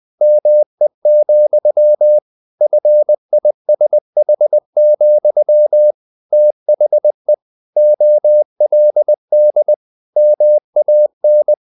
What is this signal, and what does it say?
Morse code, heard as a keyed tone, about 20 wpm, 600 Hz. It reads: ME, FISH, THE OLD MAN
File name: the old man and the sea part 2 20wpm